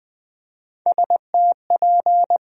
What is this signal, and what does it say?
Morse code, heard as a keyed tone, about 20 wpm, 700 Hz. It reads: STP
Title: USA Call Sign Postfixes - 3 Letters 20wpm